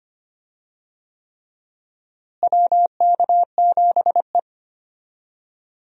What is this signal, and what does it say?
Morse code, heard as a keyed tone, about 25 wpm, 700 Hz. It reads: WK7E
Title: USA Call Signs - 4 Characters - Repeated 3x - 25wpm